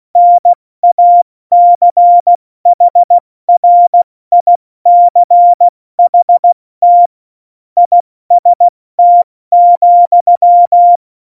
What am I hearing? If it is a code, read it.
Morse code, heard as a keyed tone, about 16 wpm, 700 Hz. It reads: NACHRICHT IST,
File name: Morsekurs / Witze / Witz 22. September 2020, 70 F80